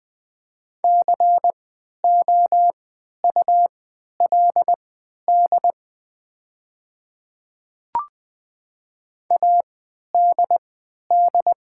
Morse code, heard as a keyed tone, about 20 wpm, spaced as for 11 wpm, 700 Hz. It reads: COULD ADD
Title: 3x Character Spacing - Top 300 Words - 20wpm